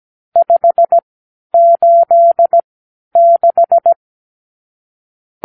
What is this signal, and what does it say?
Morse code, heard as a keyed tone, about 17 wpm, spaced as for 10 wpm, 680 Hz. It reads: 586
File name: k7qo / Lesson 54 (9)